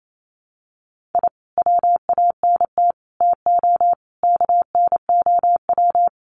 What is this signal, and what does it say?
Morse code, heard as a keyed tone, about 28 wpm, 700 Hz. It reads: I WANT TO KNOW
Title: Sets of 5 Words Encore2 28wpm